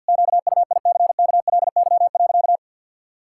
Morse code, heard as a keyed tone, about 52 wpm, 700 Hz. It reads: QUICKLY.
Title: Pangram Challenge v2 - 50wpm